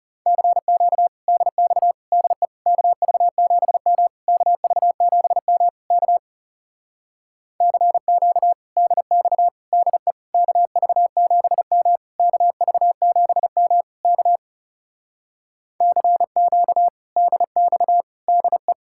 Morse code, heard as a keyed tone, about 40 wpm, 700 Hz. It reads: CQ DX DE KV7M KV7M K CQ DX DE KV7M KV7M K CQ DX DE